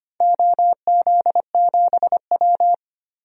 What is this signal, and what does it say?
Morse code, heard as a keyed tone, about 25 wpm, 700 Hz. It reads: OZ7W
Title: DXCC-100 Call Signs - 4 Characters - Repeated 2x - 25wpm